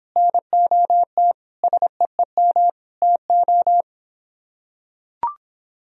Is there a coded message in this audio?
Yes: Morse code from 0.2 to 3.8 s, heard as a keyed tone, about 26 wpm, 700 Hz. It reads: NOT SEEM TO